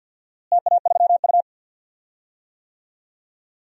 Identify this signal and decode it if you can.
Morse code, heard as a keyed tone, about 47 wpm, 700 Hz. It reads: TA3U
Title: DXCC-100 Call Signs - 4 Characters - Repeated 2x - 50wpm